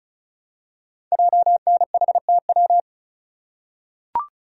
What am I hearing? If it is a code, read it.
Morse code, heard as a keyed tone, about 36 wpm, 700 Hz. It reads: JNHTW
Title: Code Groups - 5 Letters - 35wpm